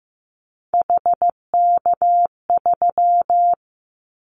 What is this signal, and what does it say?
Morse code, heard as a keyed tone, about 15 wpm, 700 Hz. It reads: HK3
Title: Callsign Prefixes for Top 100 DXCC - 15wpm